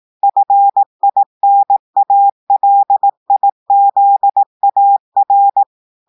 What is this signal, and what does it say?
Morse code, heard as a keyed tone, about 18 wpm, 800 Hz. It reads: FINALIZAR
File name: audios / portugais / portugais texte en clair 2 vitesse 18 mots minutes